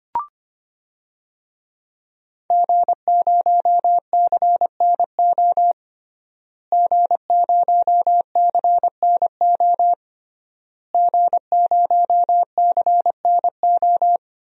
Morse code, heard as a keyed tone, about 25 wpm, 700 Hz. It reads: G0CNO G0CNO G0CNO
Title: DXCC-100 Call Signs - 5 Characters - Repeated 3x - 25wpm